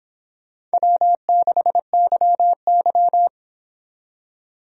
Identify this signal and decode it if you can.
Morse code, heard as a keyed tone, about 26 wpm, 700 Hz. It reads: W6YY